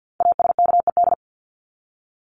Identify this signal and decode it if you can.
Morse code, heard as a keyed tone, about 50 wpm, 700 Hz. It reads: ASKED
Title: ED Suffix - Top 100 Words with Base Words 50wpm